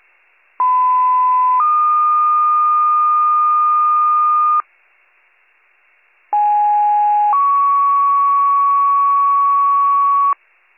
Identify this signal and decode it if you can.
Two-tone page, heard as a one-second tone, then a three-second tone, about 1010 then 1220 Hz, about 830 then 1120 Hz. It is heard from 0.6 to 10.3 s.